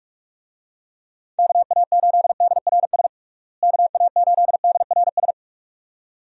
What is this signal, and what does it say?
Morse code, heard as a keyed tone, about 46 wpm, 700 Hz. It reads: KA8DRS KA8DRS
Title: USA Call Signs - 6 Characters - Repeated 2x - 45wpm